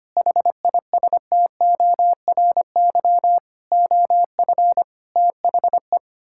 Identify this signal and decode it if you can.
Morse code, heard as a keyed tone, about 25 wpm, 700 Hz. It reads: HISTORY OF THE
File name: Sets of 9 Words - 25wpm